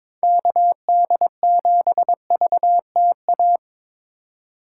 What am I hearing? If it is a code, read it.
Morse code, heard as a keyed tone, about 22 wpm, 700 Hz. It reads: KD7VTA
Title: USA Call Signs - 6 Characters - 6x Character Spacing - 22wpm